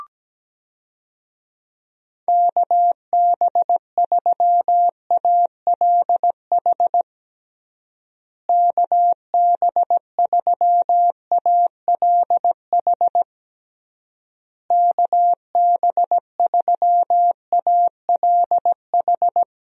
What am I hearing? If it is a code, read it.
Morse code, heard as a keyed tone, about 17 wpm, 700 Hz. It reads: KB3ALH KB3ALH KB3ALH
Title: USA Call Signs - 6 Characters - Repeated 3x - 17wpm